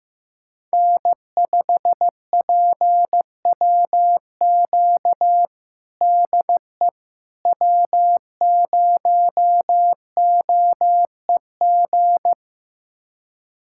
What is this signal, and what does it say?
Morse code, heard as a keyed tone, about 15 wpm, 700 Hz. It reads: N5PWQ DE W0OEG
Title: USA Callsign DE Callsign - 5 Characters - 15wpm